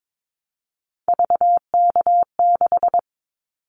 Morse code, heard as a keyed tone, about 22 wpm, 700 Hz. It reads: VK6
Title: Callsign Prefixes for Top 50 DXCC - 22wpm